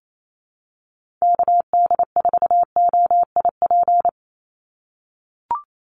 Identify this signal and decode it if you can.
Morse code, heard as a keyed tone, about 28 wpm, 700 Hz. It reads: KD4OIP
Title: USA Call Signs 28wpm